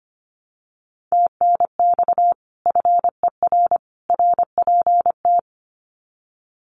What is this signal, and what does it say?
Morse code, heard as a keyed tone, about 25 wpm, 700 Hz. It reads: TNX FER RPT
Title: Common QSO Phrases 25wpm